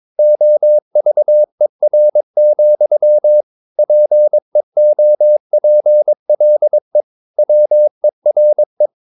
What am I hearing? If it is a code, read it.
Morse code, heard as a keyed tone, about 22 wpm, 600 Hz. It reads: OVER, PEOPLE WERE
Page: Romanen Never af Ken Follet på engelsk.